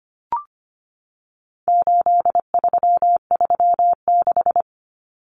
Morse code, heard as a keyed tone, about 25 wpm, 700 Hz. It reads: 8336
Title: Numbers - 4-Digit 25wpm